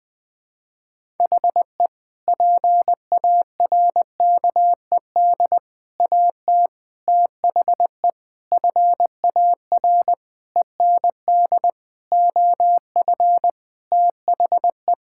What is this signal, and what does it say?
Morse code, heard as a keyed tone, about 20 wpm, 700 Hz. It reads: HE PARKED AT THE FAR END OF THE
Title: Sentences with 5-Word Phrases - Spoken Beginning - 20wpm